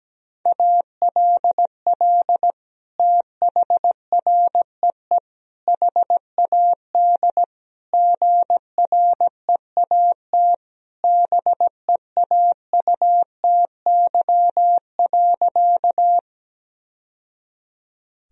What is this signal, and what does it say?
Morse code, heard as a keyed tone, about 17 wpm, 700 Hz. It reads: ALL THREE HAD GREAT BEAUTY.
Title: Sentences from Top 1000 Words - Review Part 2 - 17wpm